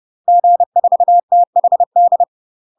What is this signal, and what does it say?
Morse code, heard as a keyed tone, about 30 wpm, 700 Hz. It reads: G4THD
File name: UK MASTER QSOs TRAINING / 30WPM / UK MASTER QSOs 0023